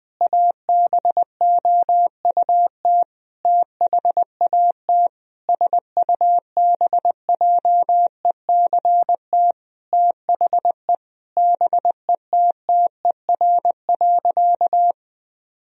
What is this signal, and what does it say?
Morse code, heard as a keyed tone, about 20 wpm, 700 Hz. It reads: ABOUT THAT SUBJECT THE BETTER.
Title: Sentences from Top 1000 Words - Review Part 2 - 20wpm